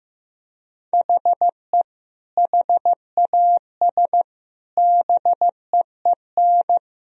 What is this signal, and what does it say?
Morse code, heard as a keyed tone, about 15 wpm, 700 Hz. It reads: HE HAS BEEN
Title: Sentences from Top 1000 Words - Review Part 1 - 15wpm